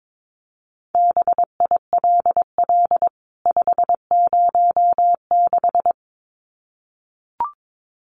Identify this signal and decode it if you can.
Morse code, heard as a keyed tone, about 22 wpm, 700 Hz. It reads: BILL 506
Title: CWT Exchange 22wpm